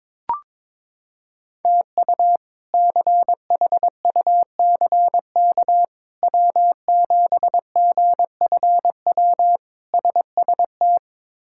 Morse code, heard as a keyed tone, about 22 wpm, 700 Hz. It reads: TU CHUCK W7GFW SST
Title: K1USN SST QSO Practice - 22wpm